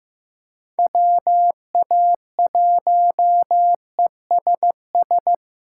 Morse code, heard as a keyed tone, about 15 wpm, 700 Hz. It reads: WA1ESS
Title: USA Call Signs - 6 Characters - 3x Character Spacing - 15wpm